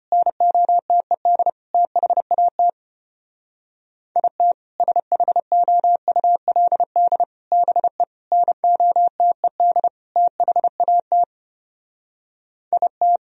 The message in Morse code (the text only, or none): NOTED THAT IT SHOULD BE NOTED THAT IT